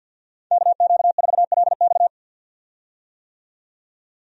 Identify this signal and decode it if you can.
Morse code, heard as a keyed tone, about 47 wpm, 700 Hz. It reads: KQ4LX
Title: USA Call Signs - 5 Characters - 50wpm